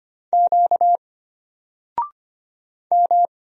Morse code, heard as a keyed tone, about 25 wpm, 700 Hz. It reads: Q M
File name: Single-Letter-Number-25wpm